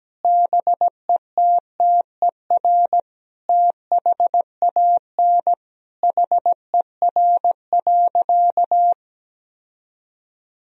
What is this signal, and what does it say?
Morse code, heard as a keyed tone, about 17 wpm, 700 Hz. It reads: BETTER THAN HER.
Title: Sentences from Top 1000 Words - Review Part 2 - 17wpm